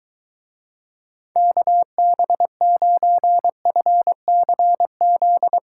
Morse code, heard as a keyed tone, about 23 wpm, 700 Hz. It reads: KB9FCZ